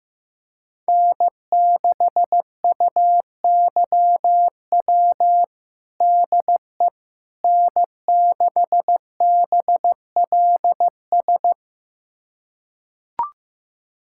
Morse code, heard as a keyed tone, about 15 wpm, 700 Hz. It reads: N6UYW DE N6BLS
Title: USA Callsign DE Callsign - 5 Characters - 15wpm